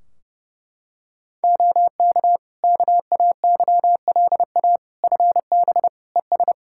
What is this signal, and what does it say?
Morse code, heard as a keyed tone, about 32 wpm, 700 Hz. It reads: OK KAYLA FB ES
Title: K1USN SST QSO Practice 30wpm